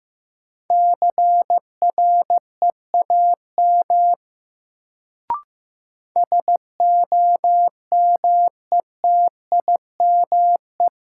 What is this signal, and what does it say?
Morse code, heard as a keyed tone, about 15 wpm, 700 Hz. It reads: CREAM SOMETIME
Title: Sentences from Top 1500 Words - New Word Drill Alt - 15wpm